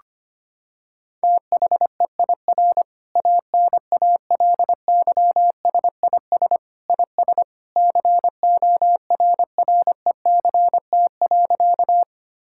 Morse code, heard as a keyed tone, about 25 wpm, 700 Hz. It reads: THEIR ANALYSIS IS CORRECT.
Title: Sentences from Top 1000 Words - Review Part 5 - 25wpm